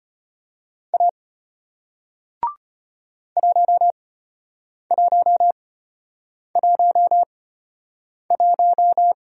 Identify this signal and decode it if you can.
Morse code, heard as a keyed tone, about 34 wpm, spaced as for 8 wpm, 700 Hz. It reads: A 1111